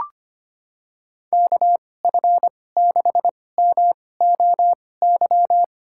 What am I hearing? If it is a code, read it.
Morse code, heard as a keyed tone, about 25 wpm, spaced as for 18 wpm, 700 Hz. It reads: KF6MOY
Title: USA Call Signs - 6 Characters - 2x Character Spacing - 25wpm